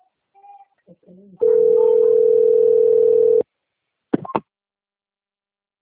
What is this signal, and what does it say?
Ringback tone (North America), 440 + 480 Hz, with speech.
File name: rg-1600-unknown- - - .50998